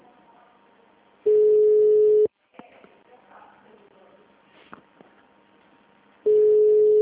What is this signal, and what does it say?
Ringback tone (Europe, 425 Hz).
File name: out- -2005- - - .66187